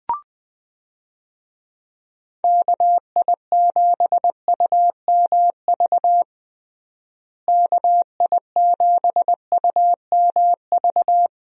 Morse code, heard as a keyed tone, about 21 wpm, 700 Hz. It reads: KI7UMV KI7UMV
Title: USA Call Signs - 6 Characters - Repeated 3x - 20wpm